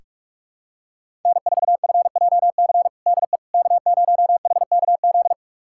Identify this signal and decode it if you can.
Morse code, heard as a keyed tone, about 46 wpm, 700 Hz. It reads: N4FJC DE K0HKZ
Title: USA Callsign DE Callsign - 5 Characters - 45wpm